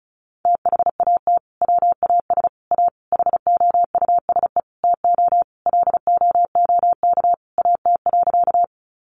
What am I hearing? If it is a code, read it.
Morse code, heard as a keyed tone, about 34 wpm, 700 Hz. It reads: THAT WAS A HOUSE TO LOOK AT.
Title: Sentences from Top 200 Words 35wpm